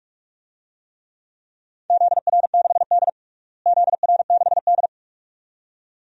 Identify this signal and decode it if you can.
Morse code, heard as a keyed tone, about 46 wpm, 700 Hz. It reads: ZR6D ZR6D
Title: DXCC-100 Call Signs - 4 Characters - Repeated 2x - 45wpm